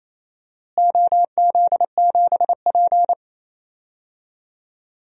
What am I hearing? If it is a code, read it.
Morse code, heard as a keyed tone, about 28 wpm, 700 Hz. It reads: OZ7P
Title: DXCC-100 Call Signs - 4 Characters - Repeated 2x - 28wpm